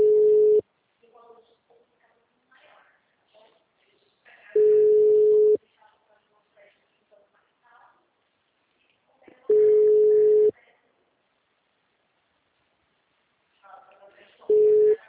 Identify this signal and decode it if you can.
Ringback tone (Europe, 425 Hz), with speech.